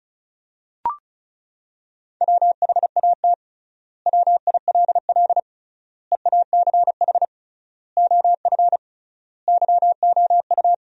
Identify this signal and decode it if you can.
Morse code, heard as a keyed tone, about 36 wpm, 700 Hz. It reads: WHAT WILL EACH OF YOU
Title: 3x Word-Spacing - Sentences from Top 200 Words - 35wpm